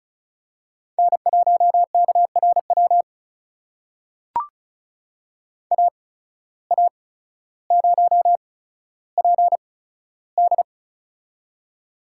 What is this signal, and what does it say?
Morse code, heard as a keyed tone, about 34 wpm, spaced as for 15 wpm, 700 Hz. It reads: N1KRW AA0PD
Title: USA Call Signs - 5 Characters - 8x Character Spacing - 35wpm